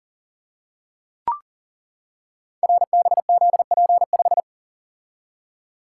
Morse code, heard as a keyed tone, about 42 wpm, 690 Hz. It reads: RBZP5
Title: Code Groups - 5 Letter:Numbers - 40wpm